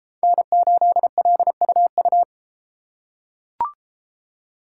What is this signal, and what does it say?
Morse code, heard as a keyed tone, about 31 wpm, 700 Hz. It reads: N8LUU